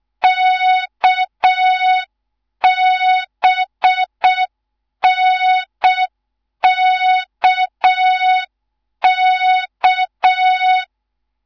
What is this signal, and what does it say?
Morse code, heard as a keyed tone, about 6 wpm, 3700 Hz. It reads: KBNKK
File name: Morze-HA2MN / hang / 11 GYAKORLAT